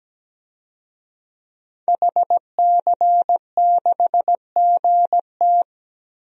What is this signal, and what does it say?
Morse code, heard as a keyed tone, about 17 wpm, 700 Hz. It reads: HC6GT
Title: DXCC-100 Call Signs - 5 Characters - Repeated 3x - 17wpm